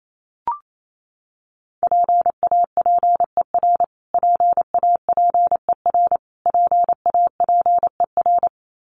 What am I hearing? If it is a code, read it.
Morse code, heard as a keyed tone, about 28 wpm, 700 Hz. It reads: PAPER PAPER PAPER
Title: Top 300 Words Repeated 28wpm